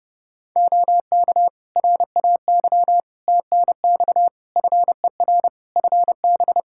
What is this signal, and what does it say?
Morse code, heard as a keyed tone, about 30 wpm, 700 Hz. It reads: OK RAY TNX FER FB
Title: Practice QSO - Exchange 4 - Ending - 30wpm